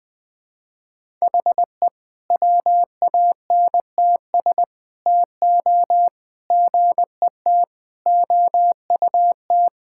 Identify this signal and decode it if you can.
Morse code, heard as a keyed tone, about 20 wpm, 700 Hz. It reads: HE WANTS TO GET OUT
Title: Sentences with 4-Word Phrases - 20wpm